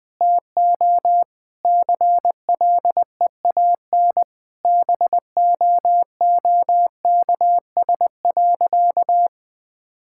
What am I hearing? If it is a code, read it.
Morse code, heard as a keyed tone, about 20 wpm, 700 Hz. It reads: TO CLEAN BOOKS.
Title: Sentences with 4-Word Phrases - 20wpm